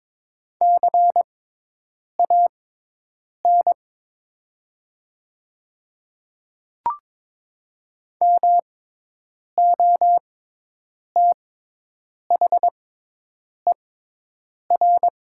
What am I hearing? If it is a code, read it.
Morse code, heard as a keyed tone, about 22 wpm, spaced as for 8 wpm, 700 Hz. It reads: CAN MOTHER
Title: 6x Character Spacing - Top 200 Words - 22wpm